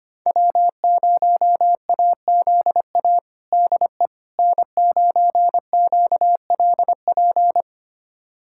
Morse code, heard as a keyed tone, about 25 wpm, 700 Hz. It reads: W0AZA DE N9QLP
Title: USA Callsign DE Callsign - 5 Characters - 25wpm